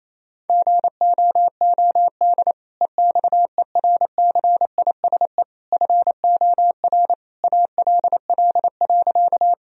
Morse code, heard as a keyed tone, about 27 wpm, 700 Hz. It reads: GOOD EXERCISE FOR ALL.
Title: Sentences from Top 1000 Words - Review Part 4 - 28wpm